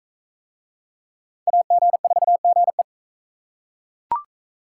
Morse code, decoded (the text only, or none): AG4GE